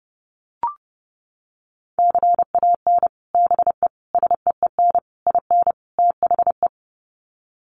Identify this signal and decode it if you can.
Morse code, heard as a keyed tone, about 30 wpm, 700 Hz. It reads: CAN BE SEEN IN THE
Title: Sets of 5 Words Encore2 30wpm